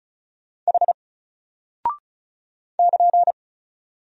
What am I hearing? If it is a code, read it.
Morse code, heard as a keyed tone, about 34 wpm, 700 Hz. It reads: H <KN>
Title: Single Character (Letter:Number:Punctuation:Prosign) Alt 35wpm